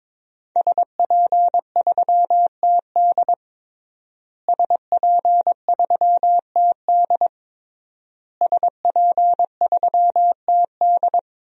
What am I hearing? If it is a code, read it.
Morse code, heard as a keyed tone, about 22 wpm, 700 Hz. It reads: SP3TD SP3TD SP3TD
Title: DXCC-100 Call Signs - 5 Characters - Repeated 3x - 22wpm